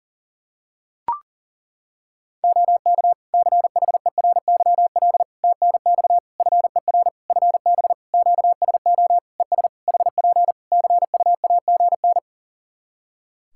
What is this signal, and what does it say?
Morse code, heard as a keyed tone, about 41 wpm, 700 Hz. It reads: OK CHERYL TNX FER FB QSO ES HP CUAGN
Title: Practice QSO - Exchange 4 - Ending - 40wpm